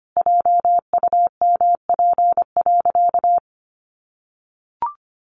Morse code, heard as a keyed tone, about 25 wpm, 700 Hz. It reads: JUMP.
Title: Sentences from Top 1300 Words 25wpm